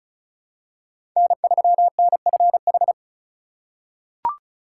Morse code, heard as a keyed tone, about 36 wpm, 700 Hz. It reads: N3NFH
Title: USA Call Signs - 5 Characters - 2x Character Spacing - 35wpm